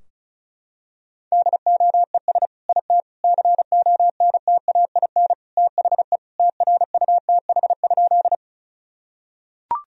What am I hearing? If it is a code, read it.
Morse code, heard as a keyed tone, about 36 wpm, 700 Hz. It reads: DOES IT CONTAIN THE TRUTH?
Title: Sentences from Top 1000 Words - Review Part 2 - 35wpm